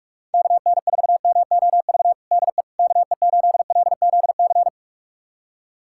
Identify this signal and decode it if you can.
Morse code, heard as a keyed tone, about 45 wpm, 700 Hz. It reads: KN4MOV DE KE8LZC
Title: USA Callsign DE Callsign - 6 Characters - 45wpm